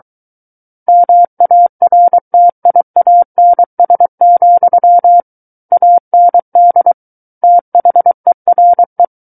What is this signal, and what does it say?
Morse code, heard as a keyed tone, about 23 wpm, 700 Hz. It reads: MARTIANS, AND THERE
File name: Book One / 14-Chapter 14 - 23 WPM